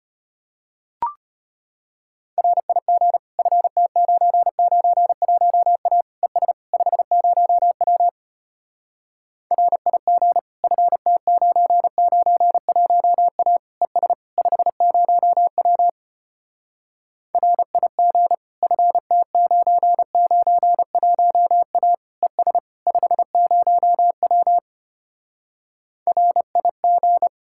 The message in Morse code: RIG FT991A ES 50W RIG FT991A ES 50W RIG FT991A ES 50W RIG

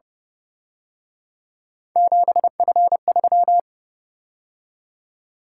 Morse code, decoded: 7F3

